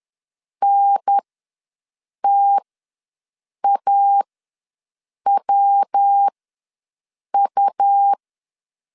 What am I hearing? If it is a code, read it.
Morse code, heard as a keyed tone, about 10 wpm, spaced as for 6 wpm, 800 Hz. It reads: NTAWU
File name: LEZIONI / 5 LBVC / 5 LBV ws 10wpm 3645